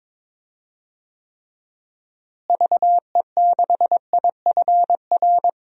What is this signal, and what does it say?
Morse code, heard as a keyed tone, about 22 wpm, 700 Hz. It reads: VE6IFR
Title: DXCC-100 Call Signs - 6 Characters - Repeated 3x - 22wpm